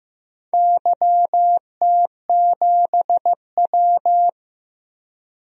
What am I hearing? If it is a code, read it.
Morse code, heard as a keyed tone, about 15 wpm, 700 Hz. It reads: YT7W
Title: DXCC-100 Call Signs - 4 Characters - 15wpm